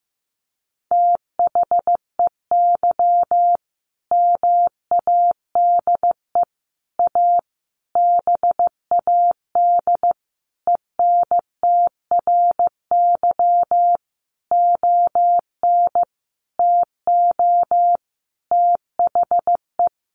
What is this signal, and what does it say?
Morse code, heard as a keyed tone, about 15 wpm, 700 Hz. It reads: THEY MADE A BAD ENTRY ON TO THE
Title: Sentences from Top 1500 Words 15wpm